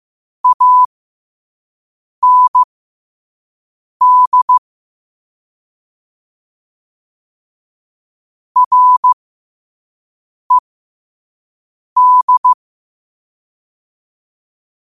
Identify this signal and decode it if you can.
Morse code, heard as a keyed tone, about 15 wpm, spaced as for 5 wpm, 1000 Hz. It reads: AND RED
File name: qotdfiles / 5wpm